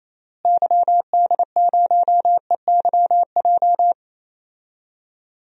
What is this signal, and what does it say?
Morse code, heard as a keyed tone, about 28 wpm, 700 Hz. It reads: YD0EYJ
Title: DXCC-100 Call Signs - 6 Characters - Repeated 2x - 28wpm